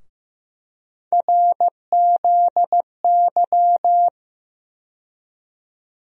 Morse code, heard as a keyed tone, about 15 wpm, 700 Hz. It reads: RZY
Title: USA Call Sign Postfixes - 3 Letters 15wpm